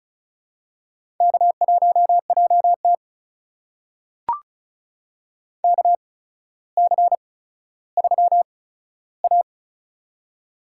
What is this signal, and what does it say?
Morse code, heard as a keyed tone, about 36 wpm, spaced as for 15 wpm, 700 Hz. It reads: K1JT KC3A
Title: USA Call Signs - 4 Characters - 8x Character Spacing - 35wpm